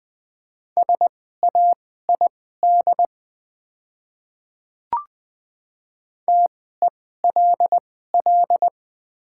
Morse code, heard as a keyed tone, about 20 wpm, spaced as for 14 wpm, 700 Hz. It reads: SAID TELL
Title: 2x Character Spacing - Top 200 Words - 20wpm